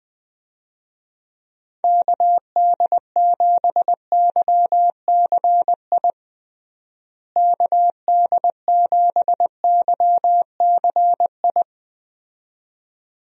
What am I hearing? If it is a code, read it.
Morse code, heard as a keyed tone, about 21 wpm, 700 Hz. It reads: KD7YCI KD7YCI
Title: USA Call Signs - 6 Characters - Repeated 2x - 20wpm